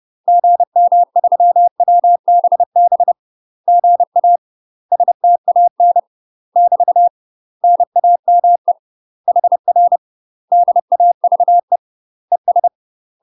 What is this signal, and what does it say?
Morse code, heard as a keyed tone, about 30 wpm, 700 Hz. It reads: GM3WBB GA STAN <BT> NAME HR DAVE ES
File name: UK MASTER QSOs TRAINING / 30WPM / UK MASTER QSOs 0014